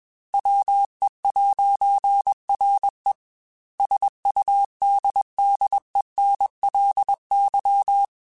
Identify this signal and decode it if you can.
Morse code, heard as a keyed tone, about 21 wpm, 800 Hz. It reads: WE'RE SUDDENLY
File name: Technical Subjects / Technical Subjects-mp3s / AND Gate @20